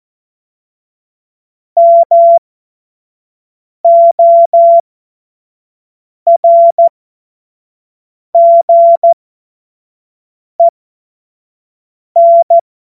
Morse code, heard as a keyed tone, about 14 wpm, spaced as for 5 wpm, 680 Hz. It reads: MORGEN
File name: Morsekurs / Witze / Witz 03. Mai 2022 25 F70